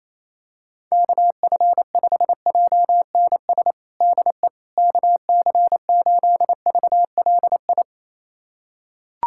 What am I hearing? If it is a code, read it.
Morse code, heard as a keyed tone, about 28 wpm, 700 Hz. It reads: KF5JNS DE KC8VLI
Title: USA Callsign DE Callsign - 6 Characters - 28wpm